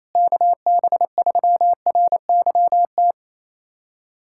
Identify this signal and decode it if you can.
Morse code, heard as a keyed tone, about 27 wpm, 700 Hz. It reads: KB3RYT
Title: USA Callsign DE Callsign - 6 Characters - 28wpm